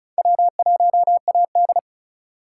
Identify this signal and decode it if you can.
Morse code, heard as a keyed tone, about 43 wpm, 700 Hz. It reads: W1AD
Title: Speed Racing - USA 1x2 Call Signs 35wpm